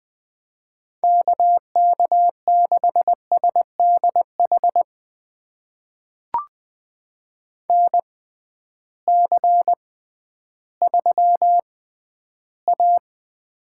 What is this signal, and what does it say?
Morse code, heard as a keyed tone, about 21 wpm, spaced as for 12 wpm, 700 Hz. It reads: KK6SDH NC3A
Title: USA Call Signs - 6x Character Spacing - 20wpm